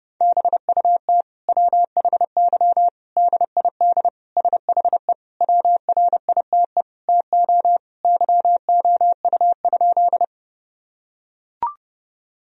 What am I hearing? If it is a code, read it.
Morse code, heard as a keyed tone, about 30 wpm, 700 Hz. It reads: BUT WHY DID SHE WRITE TO YOU?
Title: Sentences from Top 1000 Words - Review Part 1 - 30wpm